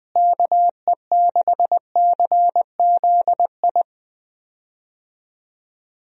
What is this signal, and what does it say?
Morse code, heard as a keyed tone, about 21 wpm, 700 Hz. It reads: KE6CZI
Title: USA Call Signs - 6 Characters - Repeated 2x - 20wpm